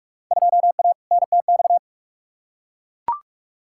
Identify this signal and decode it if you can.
Morse code, heard as a keyed tone, about 42 wpm, 700 Hz. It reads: 2A NTX
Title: Field Day Exchanges - 45wpm